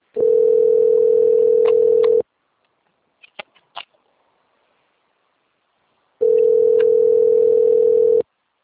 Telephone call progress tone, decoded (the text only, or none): ringback tone